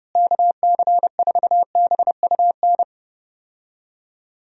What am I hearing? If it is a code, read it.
Morse code, heard as a keyed tone, about 31 wpm, 700 Hz. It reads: KC4BUN
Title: USA Call Signs - 6 Characters - Repeated 2x - 30wpm